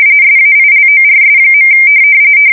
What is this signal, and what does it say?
RTTY, about 45.45 baud. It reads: SAILORS.
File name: fsk170 / USS-Missouri